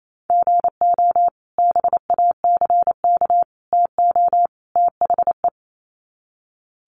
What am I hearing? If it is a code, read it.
Morse code, heard as a keyed tone, about 28 wpm, 700 Hz. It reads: GO BACK TO THE
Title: Sets of 4 Words 28wpm